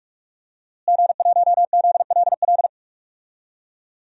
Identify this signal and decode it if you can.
Morse code, heard as a keyed tone, about 46 wpm, 700 Hz. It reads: G1ZLL
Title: DXCC-100 Call Signs - 5 Characters - Repeated 3x - 45wpm